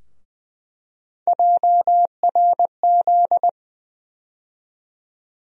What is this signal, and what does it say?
Morse code, heard as a keyed tone, about 20 wpm, 700 Hz. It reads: JRZ
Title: USA Call Sign Postfixes - 3 Letters 20wpm